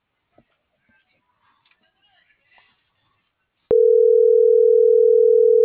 Ringback tone (North America), 440 + 480 Hz.